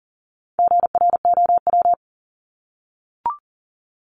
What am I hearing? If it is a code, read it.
Morse code, heard as a keyed tone, about 40 wpm, 700 Hz. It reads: GROW
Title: Top 500 Words 40wpm